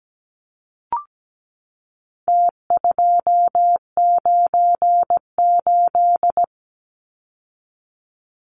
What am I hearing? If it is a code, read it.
Morse code, heard as a keyed tone, about 17 wpm, 700 Hz. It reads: T298
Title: Numbers - Contest Style - 4-Digit 17wpm